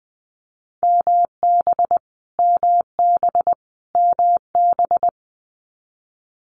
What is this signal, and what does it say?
Morse code, heard as a keyed tone, about 20 wpm, 700 Hz. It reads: MB MB MB
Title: Canadian Provinces and Territories - Repeated 3x - 20wpm